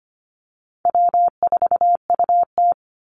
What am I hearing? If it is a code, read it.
Morse code, heard as a keyed tone, about 25 wpm, 700 Hz. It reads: W4UT
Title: Easy Call Signs Repeated 2x - 25wpm